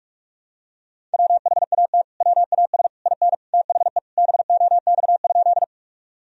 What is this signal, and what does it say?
Morse code, heard as a keyed tone, about 46 wpm, 700 Hz. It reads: WHAT WAS IN THE BOX?
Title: Sentences from Top 1000 Words - Review Part 2 - 45wpm